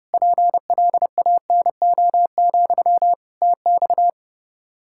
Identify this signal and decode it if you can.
Morse code, heard as a keyed tone, about 30 wpm, 700 Hz. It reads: PLANO, TX
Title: K1USN SST QSO Practice 30wpm